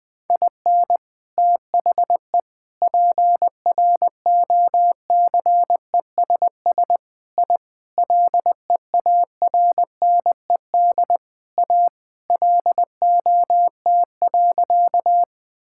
Morse code, heard as a keyed tone, about 20 wpm, 700 Hz. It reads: IN THE PROCESS I LEARNED A LOT.
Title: Sentences with 3-Word Phrases - 20wpm